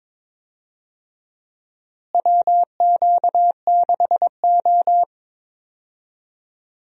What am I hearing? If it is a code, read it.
Morse code, heard as a keyed tone, about 22 wpm, 700 Hz. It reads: WQ6O
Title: USA Call Signs - 4 Characters - Repeated 3x - 22wpm